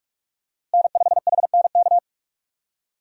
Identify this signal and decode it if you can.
Morse code, heard as a keyed tone, about 45 wpm, 700 Hz. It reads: N5HNK
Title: USA Call Signs - 5 Characters - Repeated 2x - 45wpm